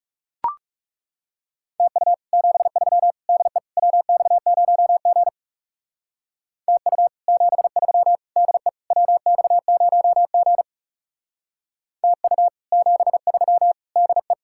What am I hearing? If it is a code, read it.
Morse code, heard as a keyed tone, about 34 wpm, 700 Hz. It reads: TU 73 DE WX0G TU 73 DE WX0G TU 73 DE
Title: Speed-Racing Common QSO Phrases - 30wpm